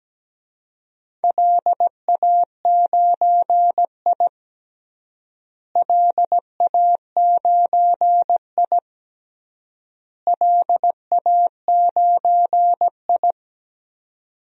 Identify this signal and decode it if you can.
Morse code, heard as a keyed tone, about 17 wpm, 700 Hz. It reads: LA9I LA9I LA9I
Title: DXCC-100 Call Signs - 4 Characters - Repeated 3x - 17wpm